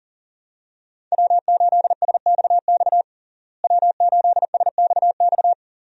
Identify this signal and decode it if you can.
Morse code, heard as a keyed tone, about 40 wpm, 700 Hz. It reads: W8SXX W8SXX
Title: USA Call Signs - 5 Characters - Repeated 3x - 40wpm